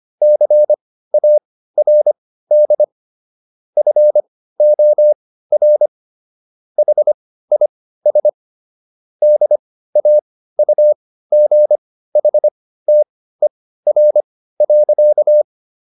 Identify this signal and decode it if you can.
Morse code, heard as a keyed tone, about 25 wpm, spaced as for 15 wpm, 600 Hz. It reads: CARD FOR HIS DAUGHTER.